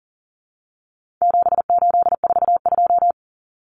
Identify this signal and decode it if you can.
Morse code, heard as a keyed tone, about 40 wpm, 700 Hz. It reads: 7842
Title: Numbers - 4-Digit 40wpm